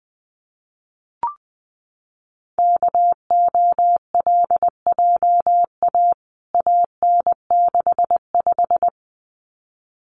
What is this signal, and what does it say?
Morse code, heard as a keyed tone, about 20 wpm, 700 Hz. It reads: KOLJA AN65
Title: CWT Exchange 20wpm